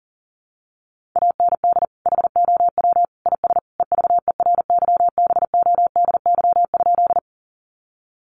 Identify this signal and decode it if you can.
Morse code, heard as a keyed tone, about 40 wpm, 700 Hz. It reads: AND HOW IS EVERYBODY?
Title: Sentences from Top 1400 Words 40wpm